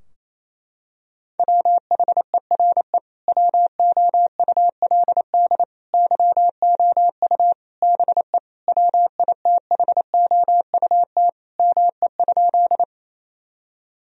Morse code, heard as a keyed tone, about 28 wpm, 700 Hz. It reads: WHERE WOULD YOU BE WITHOUT ME?
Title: Sentences from Top 1000 Words - Review Part 3 - 28wpm